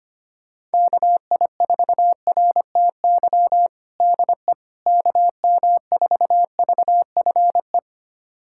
Morse code, heard as a keyed tone, about 25 wpm, 700 Hz. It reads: KI4RTY DE KM4VFE
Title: USA Callsign DE Callsign - 6 Characters - 25wpm